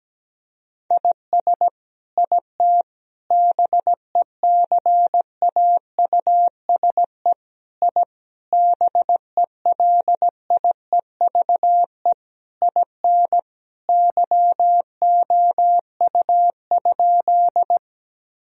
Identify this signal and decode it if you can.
Morse code, heard as a keyed tone, about 17 wpm, 700 Hz. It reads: IS IT BECAUSE I BELIEVE IN YOU?
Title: Sentences from Top 1000 Words - Review Part 3 - 17wpm